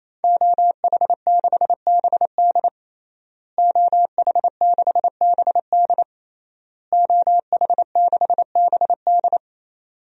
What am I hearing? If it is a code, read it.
Morse code, heard as a keyed tone, about 28 wpm, 700 Hz. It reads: OH6BD OH6BD OH6BD
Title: DXCC-100 Call Signs - 5 Characters - Repeated 3x - 28wpm